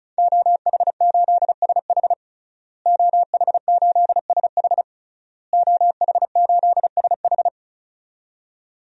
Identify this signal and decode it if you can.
Morse code, heard as a keyed tone, about 34 wpm, 700 Hz. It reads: OH8SH OH8SH OH8SH
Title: DXCC-100 Call Signs - 5 Characters - Repeated 3x - 35wpm